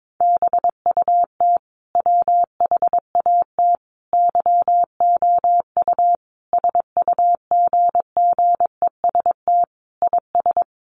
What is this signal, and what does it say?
Morse code, heard as a keyed tone, about 22 wpm, 700 Hz. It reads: BUT WHAT YOU SUGGEST IS
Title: Sentences from Top 1300 Words 22wpm